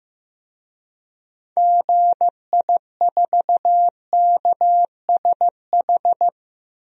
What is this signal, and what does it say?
Morse code, heard as a keyed tone, about 16 wpm, 700 Hz. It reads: GI4KSH
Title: DXCC-100 Call Signs - 6 Characters - Repeated 2x - 15wpm